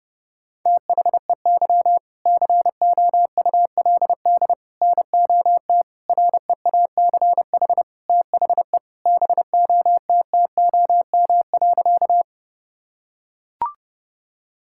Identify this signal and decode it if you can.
Morse code, heard as a keyed tone, about 30 wpm, 700 Hz. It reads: THEY COULD NOT REACH THE BOTTOM.
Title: Sentences from Top 1000 Words - Review Part 6 - 30wpm